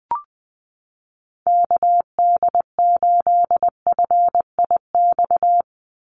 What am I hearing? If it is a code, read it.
Morse code, heard as a keyed tone, about 20 wpm, 700 Hz. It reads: KD8FIX
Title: USA Call Signs 20wpm